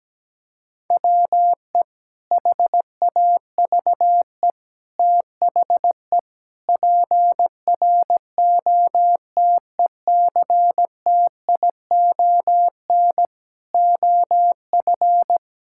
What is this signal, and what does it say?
Morse code, heard as a keyed tone, about 17 wpm, 700 Hz. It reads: WE HAVE THE PROTECTION OF
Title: Sentences from Top 1600 Words 17wpm